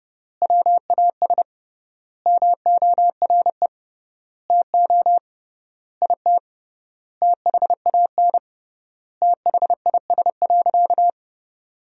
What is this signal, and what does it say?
Morse code, heard as a keyed tone, about 30 wpm, 700 Hz. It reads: WAS MORE TO IT THAN THIS.
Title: 3x Word-Spacing - Sentences from Top 100 Words - 30wpm